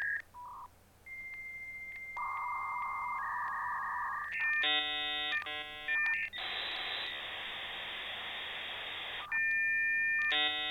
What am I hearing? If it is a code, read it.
Dial-up modem handshake.